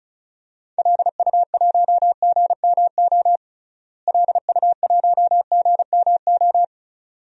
Morse code, heard as a keyed tone, about 36 wpm, 700 Hz. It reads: LU1GMO LU1GMO
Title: DXCC-100 Call Signs - 6 Characters - Repeated 2x - 35wpm